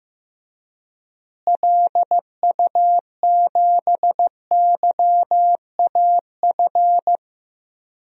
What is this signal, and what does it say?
Morse code, heard as a keyed tone, about 15 wpm, 700 Hz. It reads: LU7YAF
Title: DXCC-100 Call Signs - 6 Characters - Repeated 3x - 15wpm